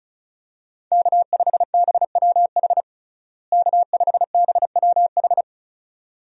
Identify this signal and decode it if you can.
Morse code, heard as a keyed tone, about 34 wpm, 700 Hz. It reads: K5BWH K5BWH
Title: USA Call Signs - 5 Characters - Repeated 2x - 35wpm